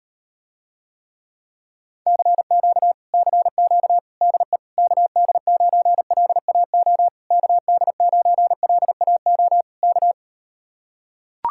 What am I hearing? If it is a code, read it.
Morse code, heard as a keyed tone, about 38 wpm, 700 Hz. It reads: CQ CQ DE KD9LAO KD9LAO K